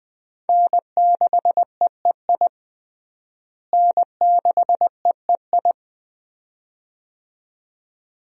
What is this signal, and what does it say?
Morse code, heard as a keyed tone, about 20 wpm, 700 Hz. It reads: N6EEI N6EEI
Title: USA Call Signs - 5 Characters - Repeated 2x - 20wpm